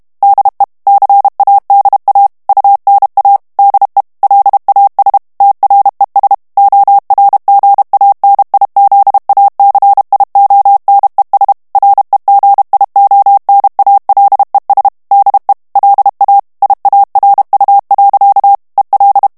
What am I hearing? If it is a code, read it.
Morse code, heard as a keyed tone, about 32 wpm, 800 Hz. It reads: DE CADA UNA DE LAS TRES ORGANIZACIONES REGIONALES DE LA IARU. EL
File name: morse mp3 / ESPAGNOL TEXTE EN CLAIR 1 VITESSE 30 MOTS MINUTES